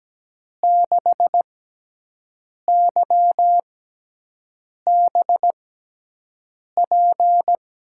Morse code, heard as a keyed tone, about 17 wpm, spaced as for 6 wpm, 700 Hz. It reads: 6YBP
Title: USA Call Signs - 6x Character Spacing - 17wpm